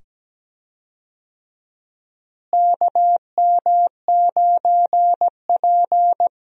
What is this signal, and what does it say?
Morse code, heard as a keyed tone, about 17 wpm, 700 Hz. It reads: KM9P
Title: USA Call Signs - 4 Characters - Repeated 2x - 17wpm